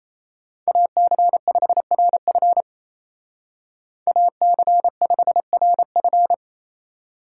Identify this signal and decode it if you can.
Morse code, heard as a keyed tone, about 33 wpm, 700 Hz. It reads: AC5RF AC5RF
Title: Speed Racing - USA 2x2 Call Signs 28wpm